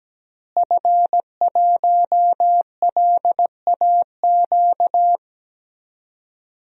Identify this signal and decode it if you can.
Morse code, heard as a keyed tone, about 17 wpm, 700 Hz. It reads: F1LAQ
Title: DXCC-100 Call Signs - 5 Characters - Repeated 2x - 17wpm